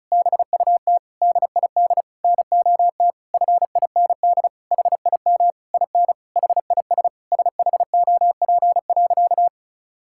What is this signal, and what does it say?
Morse code, heard as a keyed tone, about 36 wpm, 700 Hz. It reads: BUT DID NOT FIND HIM IN HIS SHOP.
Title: Sentences from Top 1600 Words 35wpm